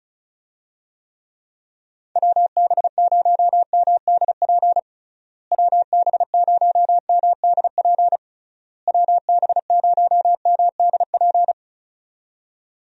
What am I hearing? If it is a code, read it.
Morse code, heard as a keyed tone, about 34 wpm, 700 Hz. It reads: WB0MDP WB0MDP WB0MDP
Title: USA Call Signs - 6 Characters - Repeated 3x - 35wpm